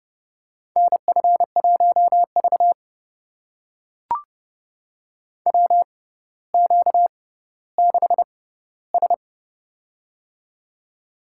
Morse code, heard as a keyed tone, about 28 wpm, spaced as for 15 wpm, 700 Hz. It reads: NF1V WQ6S
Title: USA Call Signs - 4 Characters - 6x Character Spacing - 30wpm